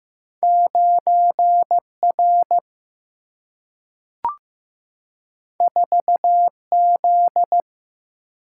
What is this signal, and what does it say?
Morse code, heard as a keyed tone, about 15 wpm, 700 Hz. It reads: 9 R 4 Z
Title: Single-Letter Single-Number 15wpm